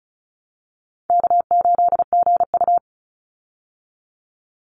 Morse code, heard as a keyed tone, about 34 wpm, 700 Hz. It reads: K8GU
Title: Easy Call Signs Repeated 2x - 35wpm